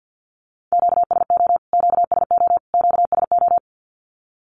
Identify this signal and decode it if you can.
Morse code, heard as a keyed tone, about 48 wpm, 700 Hz. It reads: QSO QSO QSO
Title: Q-Codes Repeated 3x 50wpm